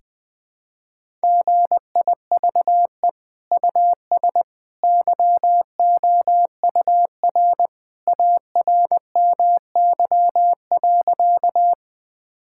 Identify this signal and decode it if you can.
Morse code, heard as a keyed tone, about 21 wpm, 700 Hz. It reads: GIVE US YOUR ARMY.
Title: Sentences from Top 1000 Words - Review Part 5 - 20wpm